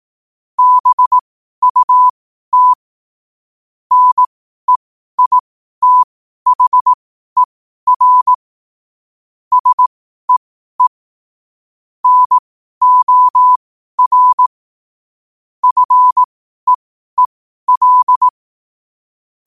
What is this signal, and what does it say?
Morse code, heard as a keyed tone, about 18 wpm, spaced as for 12 wpm, 1000 Hz. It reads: BUT NEITHER SEE NOR FEEL
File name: qotdfiles / 12wpm